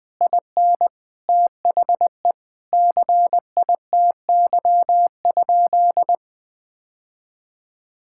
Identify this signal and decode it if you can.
Morse code, heard as a keyed tone, about 20 wpm, 700 Hz. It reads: IN THE CITY?
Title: Sentences from Top 1600 Words 20wpm